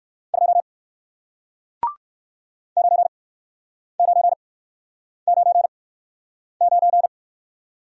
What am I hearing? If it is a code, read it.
Morse code, heard as a keyed tone, about 55 wpm, spaced as for 9 wpm, 700 Hz. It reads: 2 9999
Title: Speed-Racing Single Letter-Number 45wpm